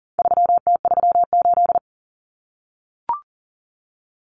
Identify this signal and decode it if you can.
Morse code, heard as a keyed tone, about 40 wpm, 700 Hz. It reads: 3T38
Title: Common Cut Numbers - 4-Digit 40wpm